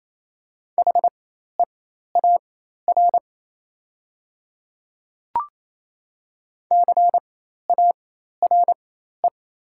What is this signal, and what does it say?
Morse code, heard as a keyed tone, about 28 wpm, spaced as for 13 wpm, 700 Hz. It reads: HEAR CARE
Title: 4x Character Spacing - Top 500 Words - 28wpm